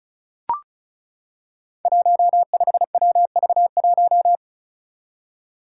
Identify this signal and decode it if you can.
Morse code, heard as a keyed tone, about 36 wpm, 700 Hz. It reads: 15WV1
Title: Five Challenging Letter-Number Combinations 35wpm